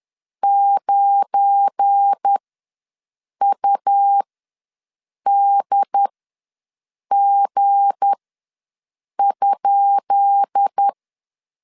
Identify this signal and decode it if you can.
Morse code, heard as a keyed tone, about 10 wpm, spaced as for 6 wpm, 800 Hz. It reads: 9UDG?
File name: LEZIONI / 10 SIGNS / 10C DOT ws 10wpm 4213